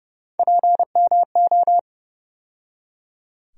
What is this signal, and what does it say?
Morse code, heard as a keyed tone, about 31 wpm, 700 Hz. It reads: PMO
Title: USA Call Sign Postfixes - 3 Letters 30wpm